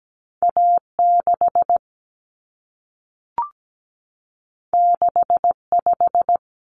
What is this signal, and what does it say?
Morse code, heard as a keyed tone, about 17 wpm, 700 Hz. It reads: A6 65
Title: Common Cut Numbers - 2-Digit 17wpm